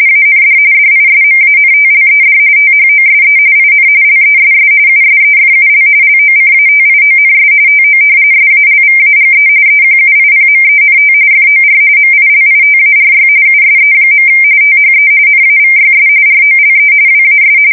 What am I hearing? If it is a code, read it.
RTTY, about 45.45 baud. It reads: AND DISMANTLED THOSE IN LESS USEFUL LOCATIONS TO SALVAGE EQUIPMENT FOR MILITARY USE. THE NAVYS ROLE WAS
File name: fsk170 / usnavy4